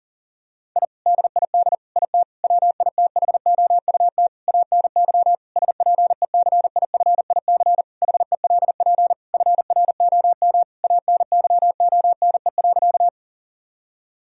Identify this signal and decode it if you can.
Morse code, heard as a keyed tone, about 37 wpm, 700 Hz. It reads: I DID IT WITHOUT ANY SPECIFIC HELP FROM ANYONE.
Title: Sentences from Top 1000 Words - Review Part 3 - 40wpm